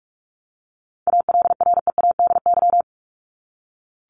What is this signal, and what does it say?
Morse code, heard as a keyed tone, about 44 wpm, 700 Hz. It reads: ALREADY
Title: Top 501-750 Words 45wpm